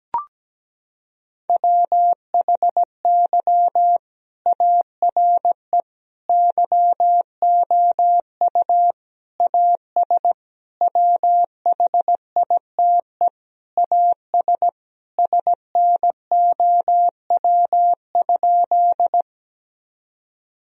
Morse code, heard as a keyed tone, about 17 wpm, 700 Hz. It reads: WHY ARE YOU AS WHITE AS SNOW?
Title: Sentences from Top 1000 Words - Review Part 3 - 17wpm